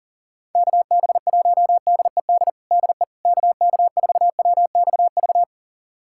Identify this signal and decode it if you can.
Morse code, heard as a keyed tone, about 40 wpm, 700 Hz. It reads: KB1DED DE KK4WXV
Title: USA Callsign DE Callsign - 6 Characters - 40wpm